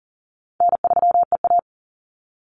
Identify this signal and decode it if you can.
Morse code, heard as a keyed tone, about 40 wpm, 700 Hz. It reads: N3EA
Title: Easy Call Signs Repeated 2x - 40wpm